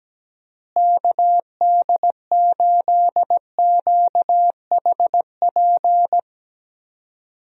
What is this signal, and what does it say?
Morse code, heard as a keyed tone, about 17 wpm, 700 Hz. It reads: KD8QHP
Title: USA Call Signs - 6 Characters - Repeated 2x - 17wpm